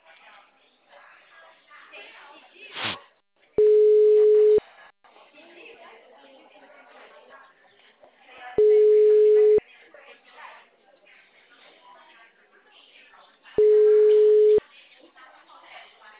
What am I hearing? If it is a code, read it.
Ringback tone (Europe, 425 Hz), with speech.